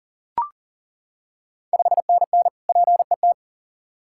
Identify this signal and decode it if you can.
Morse code, heard as a keyed tone, about 40 wpm, 690 Hz. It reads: 5NN PET
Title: Ontario QSO Party Exchange 40wpm